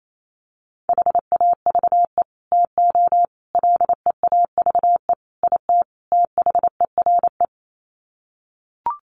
Morse code, heard as a keyed tone, about 28 wpm, 700 Hz. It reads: HAVE TO LEAVE IT THERE
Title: Sets of 5 Words Encore2 28wpm